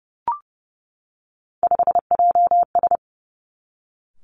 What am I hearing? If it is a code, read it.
Morse code, heard as a keyed tone, about 31 wpm, 700 Hz. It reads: 5JS
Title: Three Challenging Letter-Number Combinations 30wpm